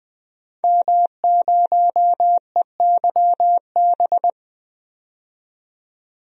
Morse code, heard as a keyed tone, about 20 wpm, 700 Hz. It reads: M0EYB
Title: DXCC-100 Call Signs - 5 Characters - 20wpm